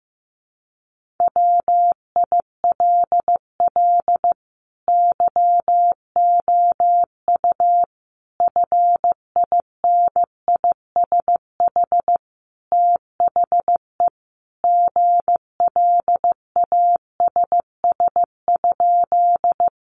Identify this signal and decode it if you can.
Morse code, heard as a keyed tone, about 15 wpm, 700 Hz. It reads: WILL YOU FINISH THE GLASS?
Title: Sentences from Top 1300 Words 15wpm